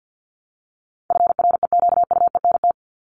Morse code, heard as a keyed tone, about 50 wpm, 700 Hz. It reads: FREQUENT
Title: LY Suffix - Top 100 Words with Base Words 50wpm